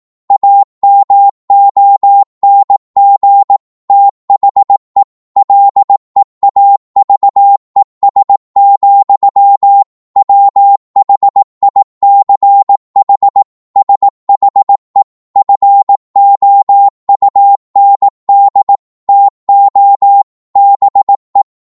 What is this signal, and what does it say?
Morse code, heard as a keyed tone, about 18 wpm, 800 Hz. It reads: AMONG THE LEAVES, WHICH SHE FOUND TO BE
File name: 18wpm